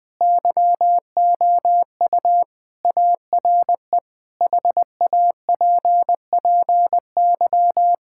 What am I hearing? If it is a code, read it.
Morse code, heard as a keyed tone, about 19 wpm, 700 Hz. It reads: YOU ARE HAPPY
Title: Sentences with 4-Word Phrases - Spoken Beginning - 20wpm